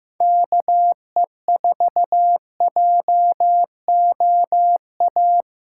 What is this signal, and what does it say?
Morse code, heard as a keyed tone, about 16 wpm, 700 Hz. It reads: KE4JOA
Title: USA Callsign DE Callsign - 6 Characters - 15wpm